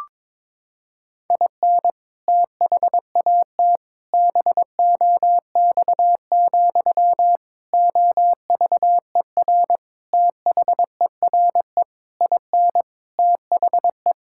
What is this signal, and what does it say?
Morse code, heard as a keyed tone, about 22 wpm, 700 Hz. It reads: IN THAT BOX, OVER THERE IN THE
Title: Sentences from Top 1600 Words 22wpm